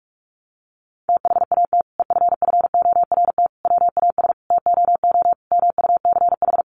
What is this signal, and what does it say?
Morse code, heard as a keyed tone, about 44 wpm, 700 Hz. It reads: THAT EFFORT WAS TOO MUCH
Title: Sentences from Top 1200 Words 45wpm